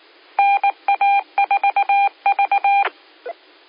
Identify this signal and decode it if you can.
Morse code, heard as a keyed tone, about 19 wpm, 800 Hz. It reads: NA4V
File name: qsorec SimplexLogic 17-53-20